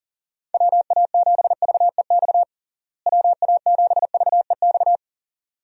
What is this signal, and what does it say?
Morse code, heard as a keyed tone, about 37 wpm, 700 Hz. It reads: WA7VEX WA7VEX
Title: USA Call Signs - 6 Characters - Repeated 2x - 40wpm